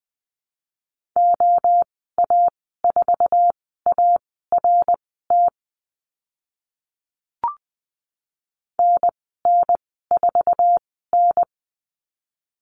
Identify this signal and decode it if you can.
Morse code, heard as a keyed tone, about 20 wpm, spaced as for 14 wpm, 700 Hz. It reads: OA4ART NN4N
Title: 055 Lesson 10g - Letter r - Call Signs 20wpm